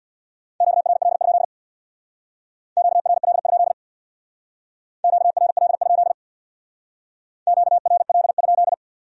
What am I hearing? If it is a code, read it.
Morse code, heard as a keyed tone, about 61 wpm, 700 Hz. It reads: QRL? QRL? QRL? QRL?
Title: Speed-Racing Q-Codes - 50wpm